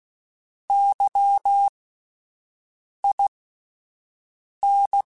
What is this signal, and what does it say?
Morse code, heard as a keyed tone, about 16 wpm, spaced as for 5 wpm, 800 Hz. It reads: YIN
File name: Misc / Misc-mp3s / WordPairs sp @10